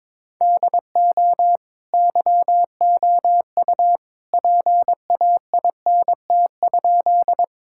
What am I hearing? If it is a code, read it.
Morse code, heard as a keyed tone, about 22 wpm, 700 Hz. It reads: DO YOU PAINT?
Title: Sentences from Top 1000 Words - Review Part 6 - 22wpm